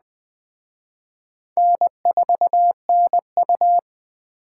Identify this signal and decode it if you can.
Morse code, heard as a keyed tone, about 23 wpm, 700 Hz. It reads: N4NU
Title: Speed Racing - USA 1x2 Call Signs 20wpm